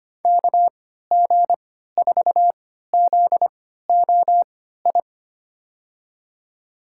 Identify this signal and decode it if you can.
Morse code, heard as a keyed tone, about 25 wpm, spaced as for 14 wpm, 700 Hz. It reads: KG4ZOI
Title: USA Call Signs - 6 Characters - 3x Character Spacing - 25wpm